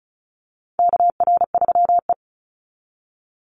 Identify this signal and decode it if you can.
Morse code, heard as a keyed tone, about 34 wpm, 700 Hz. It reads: KR3E
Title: Call Signs - 35wpm